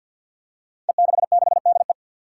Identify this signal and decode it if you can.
Morse code, heard as a keyed tone, about 50 wpm, 700 Hz. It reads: E66DE
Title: Five Challenging Letter-Number Combinations 50wpm